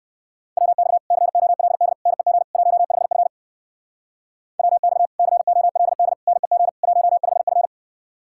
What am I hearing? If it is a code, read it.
Morse code, heard as a keyed tone, about 58 wpm, 700 Hz. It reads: WX COLD NEG 15F WX COLD NEG 15F